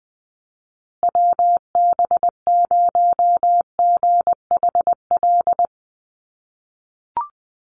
Morse code, heard as a keyed tone, about 20 wpm, 700 Hz. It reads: WB0GHL
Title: USA Call Signs Repeated 2x 20wpm